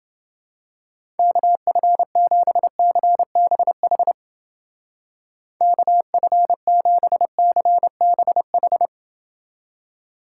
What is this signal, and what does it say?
Morse code, heard as a keyed tone, about 29 wpm, 700 Hz. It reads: KF7CBH KF7CBH